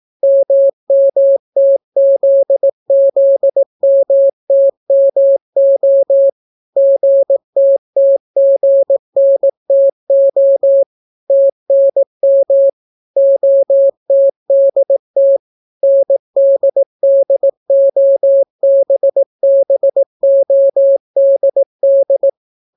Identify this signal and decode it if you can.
Morse code, heard as a keyed tone, about 18 wpm, 550 Hz. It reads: MMTZZMTMO GTTGNTO TNM OTDT NDDOBBODD